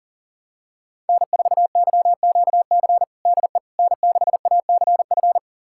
Morse code, heard as a keyed tone, about 40 wpm, 700 Hz. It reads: N4YQC DE N6ACF
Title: USA Callsign DE Callsign - 5 Characters - 40wpm